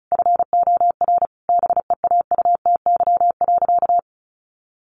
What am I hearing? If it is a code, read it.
Morse code, heard as a keyed tone, about 34 wpm, 700 Hz. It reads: FOR BEAUTY.
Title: Sentences from Top 1500 Words 35wpm